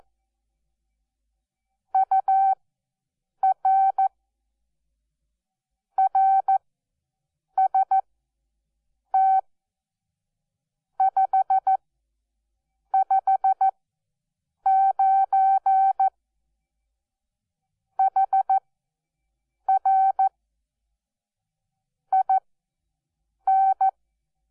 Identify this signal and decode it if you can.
Morse code, heard as a keyed tone, about 14 wpm, spaced as for 6 wpm, 770 Hz. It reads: UR RST 559 HR IN
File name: Onlinelehrgang / morsen / DJ4UF Morsen12